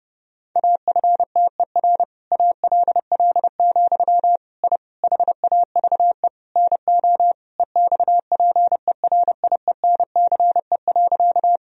Morse code, heard as a keyed tone, about 30 wpm, 700 Hz. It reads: AFTER ALL, I HAVE NO EXPERIENCE.
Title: Sentences from Top 1000 Words - Review Part 3 - 30wpm